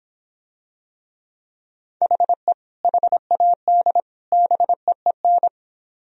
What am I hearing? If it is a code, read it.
Morse code, heard as a keyed tone, about 26 wpm, 700 Hz. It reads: HE HAD BEEN